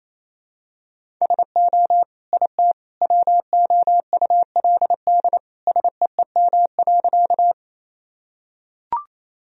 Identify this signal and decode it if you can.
Morse code, heard as a keyed tone, about 28 wpm, 700 Hz. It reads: SO IT WOULD SEEM.
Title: Sentences from Top 1000 Words - Review Part 1 - 28wpm